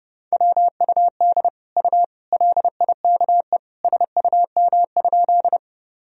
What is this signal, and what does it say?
Morse code, heard as a keyed tone, about 32 wpm, 700 Hz. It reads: WUD U LIKE SUM?
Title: Sentences from Top 100 Words and Abbreviations 30wpm